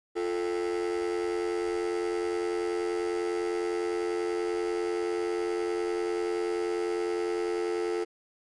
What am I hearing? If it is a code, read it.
Dial tone (North America), 350 + 440 Hz, with noise.
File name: stock music / dial tone